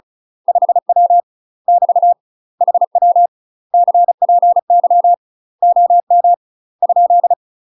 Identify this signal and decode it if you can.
Morse code, heard as a keyed tone, about 34 wpm, 700 Hz. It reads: 5W <BT> HW CPY OM ?
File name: UK MASTER QSOs TRAINING / 35WPM / UK MASTER QSOs 0004